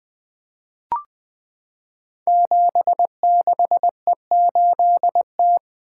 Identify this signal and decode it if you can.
Morse code, heard as a keyed tone, about 20 wpm, 700 Hz. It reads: 76E8T
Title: Code Groups - 5 Letter:Numbers - 20wpm